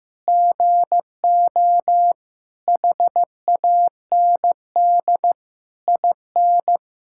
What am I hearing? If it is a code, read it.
Morse code, heard as a keyed tone, about 15 wpm, 700 Hz. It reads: GO HAND IN
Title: Sentences from Top 1000 Words - Review Part 5 - 15wpm